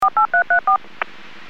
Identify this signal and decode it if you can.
DTMF, heard as telephone touch tones, tones of about 110 ms. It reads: 48334
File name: WPS / 01895 13-35 WPS 451.25 WPS